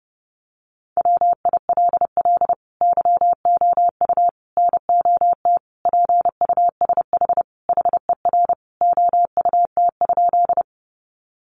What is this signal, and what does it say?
Morse code, heard as a keyed tone, about 30 wpm, 700 Hz. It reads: WILL YOU NOT PUSH HER OUT?
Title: Sentences from Top 1300 Words 30wpm